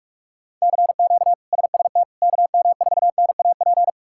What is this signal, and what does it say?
Morse code, heard as a keyed tone, about 43 wpm, 700 Hz. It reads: CQ SST KM4NAP
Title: K1USN SST QSO Practice - 45wpm